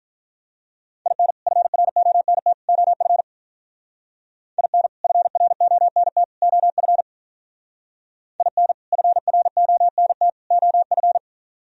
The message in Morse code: IN FRONT OF IN FRONT OF IN FRONT OF

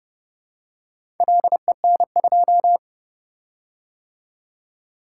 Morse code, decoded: LEN2

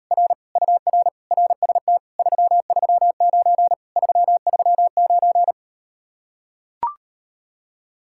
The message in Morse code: R UR RST 339 339

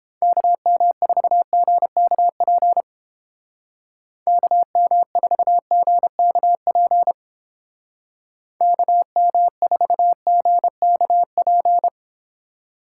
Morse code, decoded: KM4GKP KM4GKP KM4GKP

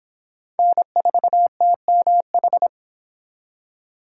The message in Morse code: N4TMH

